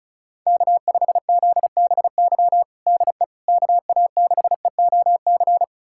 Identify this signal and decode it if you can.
Morse code, heard as a keyed tone, about 36 wpm, 700 Hz. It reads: K5ZBY DE KA6EOC
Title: USA Callsign DE Callsign 35wpm